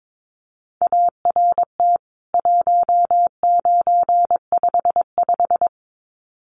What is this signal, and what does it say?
Morse code, heard as a keyed tone, about 22 wpm, 700 Hz. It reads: ART 1955
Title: CWT Exchange 22wpm